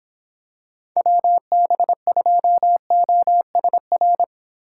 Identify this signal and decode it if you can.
Morse code, heard as a keyed tone, about 25 wpm, 700 Hz. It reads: WB2OSR